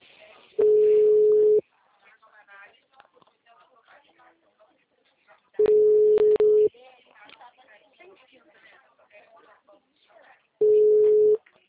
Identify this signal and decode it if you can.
Ringback tone (Europe, 425 Hz), with speech.